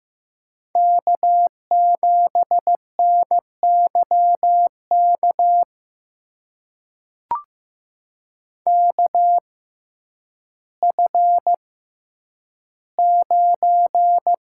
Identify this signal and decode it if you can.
Morse code, heard as a keyed tone, about 15 wpm, spaced as for 9 wpm, 700 Hz. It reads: K7NYK KF9
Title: USA Call Signs - 5 Characters - 6x Character Spacing - 15wpm